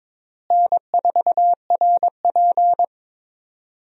Morse code, heard as a keyed tone, about 22 wpm, 700 Hz. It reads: N4RP
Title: USA Call Signs - 4 Characters - Repeated 3x - 22wpm